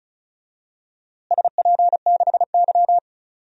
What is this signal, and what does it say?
Morse code, heard as a keyed tone, about 34 wpm, 700 Hz. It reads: SP6Y
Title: DXCC-100 Call Signs - 4 Characters - Repeated 2x - 35wpm